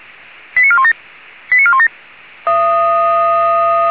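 Five-tone selective call (ZVEI), heard as quick runs of five tones, tones of about 70 ms. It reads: 89417 89417
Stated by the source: ZVEI five-tone selective call received by SDR.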